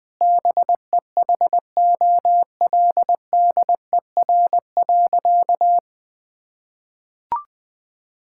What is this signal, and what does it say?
Morse code, heard as a keyed tone, about 20 wpm, 700 Hz. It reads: BEHOLDER.
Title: Sentences with 7-Word Phrases - 20wpm